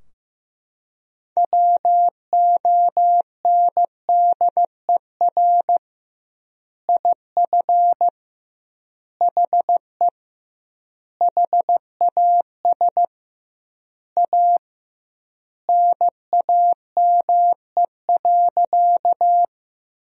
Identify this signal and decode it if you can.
Morse code, heard as a keyed tone, about 15 wpm, 700 Hz. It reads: WONDER IF HE HAS A NAME.
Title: 2x Word-Spacing - Sentences from Top 500 Words - 15wpm